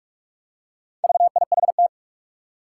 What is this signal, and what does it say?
Morse code, heard as a keyed tone, about 44 wpm, 700 Hz. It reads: VIHT
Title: Four Challenging Letter-Number Combinations 45wpm